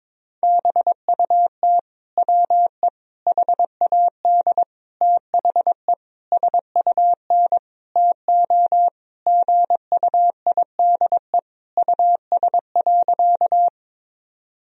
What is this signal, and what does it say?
Morse code, heard as a keyed tone, about 22 wpm, 700 Hz. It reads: BUT WE HAD THE SUN TO GUIDE US.
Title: Sentences from Top 1000 Words - Review Part 3 - 22wpm